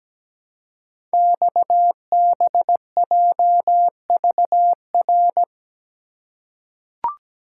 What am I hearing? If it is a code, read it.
Morse code, heard as a keyed tone, about 17 wpm, 700 Hz. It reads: XBJVR
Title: Code Groups - 5 Letters - 17wpm